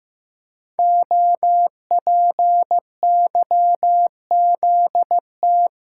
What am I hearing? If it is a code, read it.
Morse code, heard as a keyed tone, about 15 wpm, 700 Hz. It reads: OPYZT
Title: Code Groups - 5 Letters - 15wpm